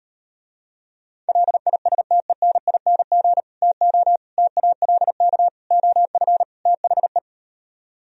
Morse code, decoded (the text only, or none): LISTENING TO TALK OF THE